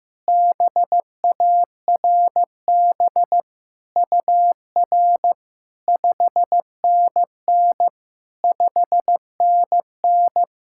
Morse code, heard as a keyed tone, about 15 wpm, 700 Hz. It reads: BARB UR 5NN 5NN
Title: SOTA QSO Practice 15wpm